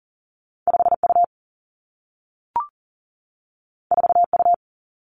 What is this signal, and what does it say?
Morse code, heard as a keyed tone, about 42 wpm, 700 Hz. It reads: 5 U 4 U
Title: Two Challenging Letter-Number Combinations 40wpm